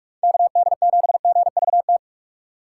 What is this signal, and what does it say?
Morse code, heard as a keyed tone, about 47 wpm, 700 Hz. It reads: KD7GVT
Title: USA Call Signs - 6 Characters - 3x Character Spacing - 45wpm